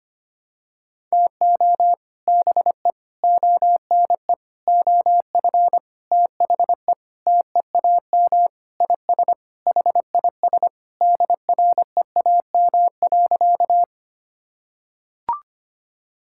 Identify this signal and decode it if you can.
Morse code, heard as a keyed tone, about 25 wpm, 700 Hz. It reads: TO BE ONE OF THE TEAM IS HIS DREAM.
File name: practice / 25wpm / Sentences with 4-Word Phrases - 25wpm